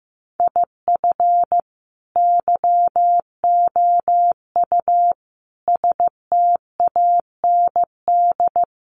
Morse code, heard as a keyed tone, about 15 wpm, 700 Hz. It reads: IF YOU STAND
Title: Sentences from Top 1200 Words 15wpm